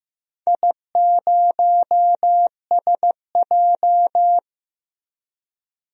Morse code, heard as a keyed tone, about 15 wpm, 700 Hz. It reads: I0SJ
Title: DXCC-100 Call Signs - 4 Characters - 15wpm